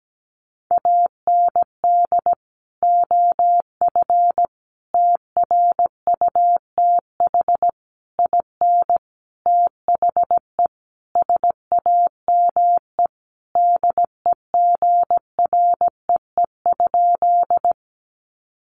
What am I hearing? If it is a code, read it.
Morse code, heard as a keyed tone, about 17 wpm, 700 Hz. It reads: AND OF TRUTH IN THE SAME DEGREE?
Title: Sentences from Top 1500 Words 17wpm